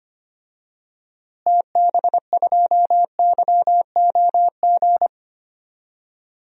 Morse code, heard as a keyed tone, about 25 wpm, 700 Hz. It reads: TB2YOG
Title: DXCC-100 Call Signs - 6 Characters - Repeated 2x - 25wpm